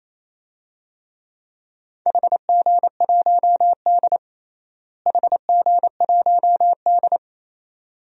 Morse code, heard as a keyed tone, about 28 wpm, 700 Hz. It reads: HG1D HG1D
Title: DXCC-100 Call Signs - 4 Characters - Repeated 2x - 28wpm